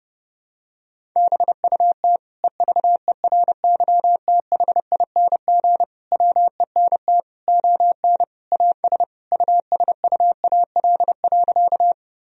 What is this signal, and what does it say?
Morse code, heard as a keyed tone, about 32 wpm, 700 Hz. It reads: BUT EVERYTHING WENT ON AS USUAL.
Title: Sentences from Top 1000 Words - Review Part 2 - 30wpm